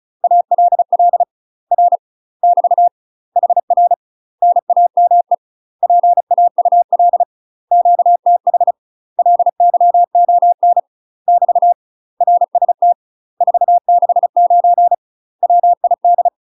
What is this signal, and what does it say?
Morse code, heard as a keyed tone, about 33 wpm, 700 Hz. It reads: ALL R <BT> HR NAME PAUL QTH LYON <BT> RST 469 WID
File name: UK MASTER QSOs TRAINING / 35WPM / UK MASTER QSOs 0038